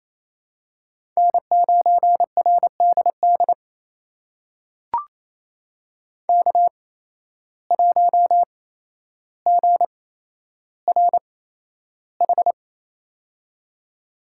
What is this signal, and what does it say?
Morse code, heard as a keyed tone, about 28 wpm, spaced as for 12 wpm, 700 Hz. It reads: N9RDD K1GRH
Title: USA Call Signs - 5 Characters - 8x Character Spacing - 28wpm